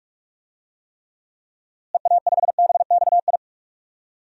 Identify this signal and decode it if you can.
Morse code, heard as a keyed tone, about 42 wpm, 700 Hz. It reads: EA5BXI
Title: DXCC-100 Call Signs - 6 Characters - Repeated 3x - 45wpm